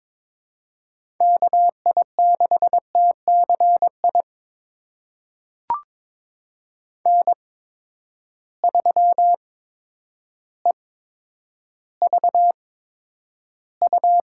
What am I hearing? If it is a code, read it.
Morse code, heard as a keyed tone, about 22 wpm, spaced as for 10 wpm, 700 Hz. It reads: KI6TCI N3EVU
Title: USA Call Signs - 8x Character Spacing - 22wpm